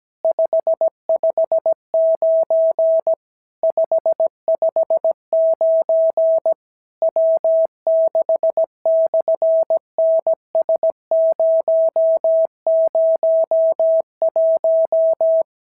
Morse code, heard as a keyed tone, about 17 wpm, 650 Hz. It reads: 559 559 W6/NS001
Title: SOTA QSO Practice 17wpm